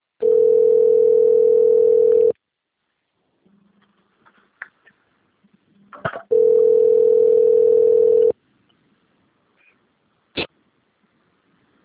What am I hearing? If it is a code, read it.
Ringback tone (North America), 440 + 480 Hz.
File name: rg-1600-unknown- - - .13137